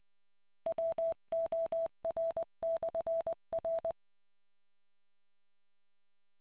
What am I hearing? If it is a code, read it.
Morse code, heard as a keyed tone, about 20 wpm, 660 Hz. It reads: WOR/R